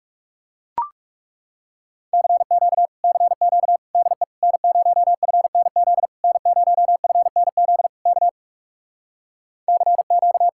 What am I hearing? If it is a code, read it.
Morse code, heard as a keyed tone, about 37 wpm, 700 Hz. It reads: CQ CQ DE N0FNZ N0FNZ K CQ
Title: Speed-Racing Common QSO Phrases - 30wpm